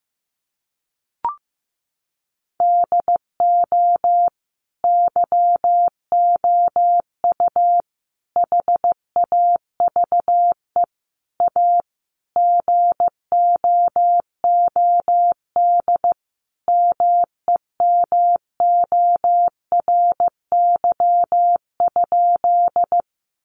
Morse code, heard as a keyed tone, about 14 wpm, 700 Hz. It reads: DO YOU HAVE A GOOD MEMORY?
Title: Sentences from Top 1300 Words 15wpm